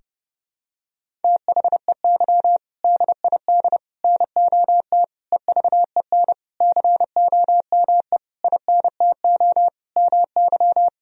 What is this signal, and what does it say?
Morse code, heard as a keyed tone, about 30 wpm, 700 Hz. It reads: THEY DID NOT EVEN COME INTO MY
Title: Sentences from Top 1000 Words - Review Part 1 - 30wpm